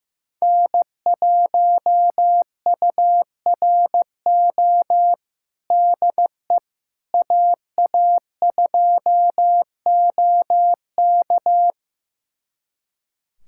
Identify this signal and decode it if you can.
Morse code, heard as a keyed tone, about 15 wpm, 700 Hz. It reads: N1URO DE AA2OK
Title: USA Callsign DE Callsign - 5 Characters - 15wpm